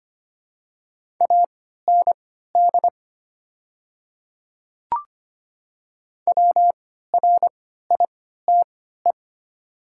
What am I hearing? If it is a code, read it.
Morse code, heard as a keyed tone, about 25 wpm, spaced as for 14 wpm, 700 Hz. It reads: AND WRITE
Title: 3x Character Spacing - Top 100 Words - 25wpm